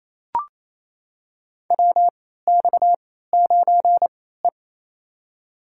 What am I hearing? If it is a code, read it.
Morse code, heard as a keyed tone, about 28 wpm, spaced as for 16 wpm, 700 Hz. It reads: WX9E
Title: USA Call Signs - 4 Characters - 3x Character Spacing - 28wpm